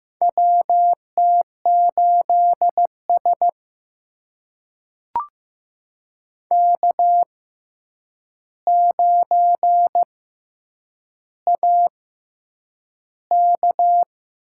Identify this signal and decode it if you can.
Morse code, heard as a keyed tone, about 15 wpm, spaced as for 8 wpm, 700 Hz. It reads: WT8S K9AK
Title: USA Call Signs - 4 Characters - 6x Character Spacing - 15wpm